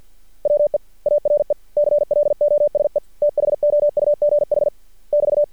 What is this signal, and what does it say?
Morse code, heard as a keyed tone, about 50 wpm, 590 Hz. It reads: WE ARE CLOSE THOUGH <BT>
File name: archived / CRSnet